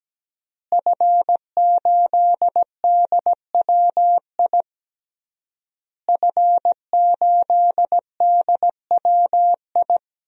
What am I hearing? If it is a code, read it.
Morse code, heard as a keyed tone, about 17 wpm, 700 Hz. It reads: F8DWI F8DWI
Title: DXCC-100 Call Signs - 5 Characters - Repeated 3x - 17wpm